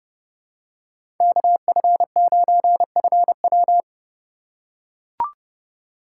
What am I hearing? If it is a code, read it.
Morse code, heard as a keyed tone, about 30 wpm, 700 Hz. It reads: KF9FW
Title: USA Call Signs - 3x Character Spacing - 30wpm